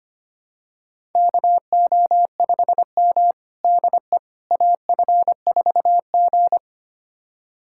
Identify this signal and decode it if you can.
Morse code, heard as a keyed tone, about 25 wpm, 700 Hz. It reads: KO5M DE AF4G
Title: USA Callsign DE Callsign - 4 Characters - 25wpm